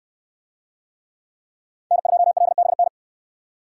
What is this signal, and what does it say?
Morse code, heard as a keyed tone, about 64 wpm, 700 Hz. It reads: N2LBN